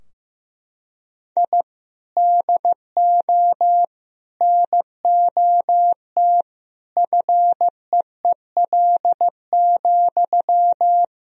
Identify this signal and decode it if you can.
Morse code, heard as a keyed tone, about 16 wpm, 700 Hz. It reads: I DO NOT FEEL,
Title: Sentences from Top 1000 Words - Review Part 3 - 15wpm